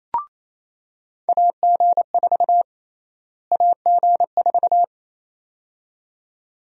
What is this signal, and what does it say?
Morse code, heard as a keyed tone, about 28 wpm, 700 Hz. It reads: AG4 AG4
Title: USA Call Sign Prefixes - Repeated 2x - 28wpm